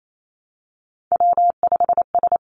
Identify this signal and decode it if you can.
Morse code, heard as a keyed tone, about 28 wpm, 700 Hz. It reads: W5S
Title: Three Challenging Letter-Number Combinations 28wpm